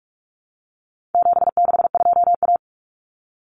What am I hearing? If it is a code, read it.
Morse code, heard as a keyed tone, about 44 wpm, 700 Hz. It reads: 762A
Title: Common Cut Numbers - 4-Digit 45wpm